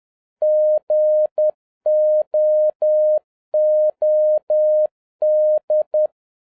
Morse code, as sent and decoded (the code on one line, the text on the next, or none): --. --- --- -..
GOOD